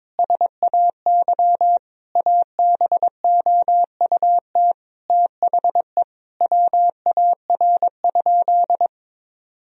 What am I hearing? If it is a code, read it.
Morse code, heard as a keyed tone, about 21 wpm, 700 Hz. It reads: SAY ABOUT THE WAR?
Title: Sentences from Top 1000 Words - Review Part 4 - 22wpm